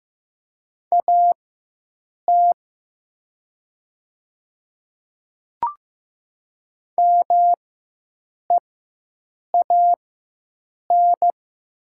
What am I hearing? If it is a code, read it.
Morse code, heard as a keyed tone, about 15 wpm, spaced as for 7 wpm, 700 Hz. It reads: AT MEAN
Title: 4x Character Spacing - Top 500 Words - 15wpm